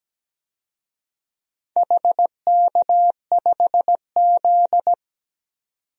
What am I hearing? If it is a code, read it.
Morse code, heard as a keyed tone, about 17 wpm, 700 Hz. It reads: HK5Z
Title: DXCC-100 Call Signs - 4 Characters - 17wpm